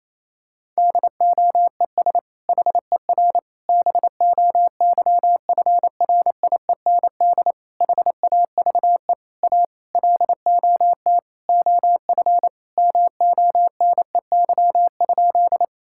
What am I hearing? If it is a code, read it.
Morse code, heard as a keyed tone, about 28 wpm, 700 Hz. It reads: DOES HER BOYFRIEND HAVE A LOT OF MONEY?
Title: Sentences from Top 1000 Words - Review Part 5 - 28wpm